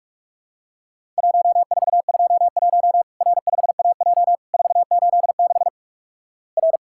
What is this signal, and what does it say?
Morse code, heard as a keyed tone, about 46 wpm, 700 Hz. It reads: 1421 R5AJ 486 R
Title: Worked All Europe DX Contest - QTC Exchange - 45wpm